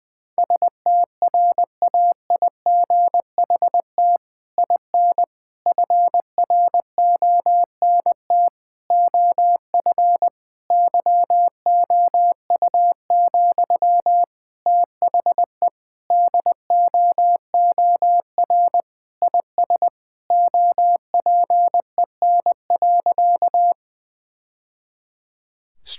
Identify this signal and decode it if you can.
Morse code, heard as a keyed tone, about 20 wpm, 700 Hz. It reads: STRAIGHT IN FRONT OF YOU, THE DOOR IS OPEN.
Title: Sentences from Top 1000 Words - Review Part 6 - 20wpm